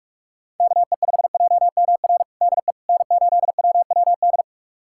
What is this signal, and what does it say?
Morse code, heard as a keyed tone, about 46 wpm, 700 Hz. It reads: KE5JMR DE N8WWD
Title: USA Callsign DE Callsign 45wpm